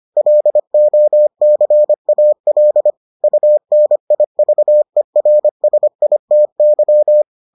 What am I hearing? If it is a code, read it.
Morse code, heard as a keyed tone, about 25 wpm, 600 Hz. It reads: LOCAL UNIVERSITY